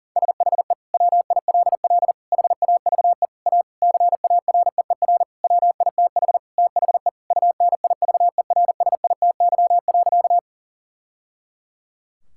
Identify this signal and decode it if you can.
Morse code, heard as a keyed tone, about 40 wpm, 700 Hz. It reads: SHE WILL HAVE A CAREER WITH THE UNIVERSITY.
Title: Sentences from Top 1000 Words - Review Part 6 - 40wpm